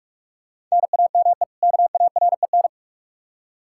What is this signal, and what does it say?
Morse code, heard as a keyed tone, about 46 wpm, 700 Hz. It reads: NAME KAREN
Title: Practice QSO - Exchange 1 - Beginning - 45wpm